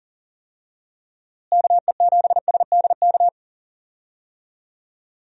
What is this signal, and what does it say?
Morse code, heard as a keyed tone, about 40 wpm, 700 Hz. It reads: KE7SDK
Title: USA Call Signs - 6 Characters - Repeated 2x - 40wpm